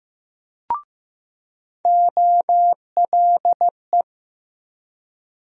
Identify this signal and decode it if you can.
Morse code, heard as a keyed tone, about 15 wpm, 700 Hz. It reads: OLE
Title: USA Call Sign Postfixes - 3 Letters 15wpm